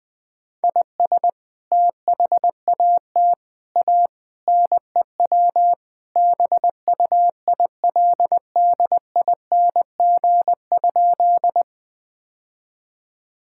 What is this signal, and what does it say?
Morse code, heard as a keyed tone, about 19 wpm, 700 Hz. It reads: IS THAT A NEW BUILDING?
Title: Sentences from Top 1000 Words - Review Part 5 - 20wpm